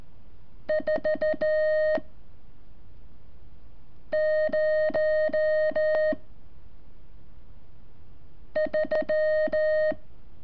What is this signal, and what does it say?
Morse code, heard as a keyed tone, about 14 wpm, 620 Hz. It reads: 4 0 3
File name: curso / ejercicio 28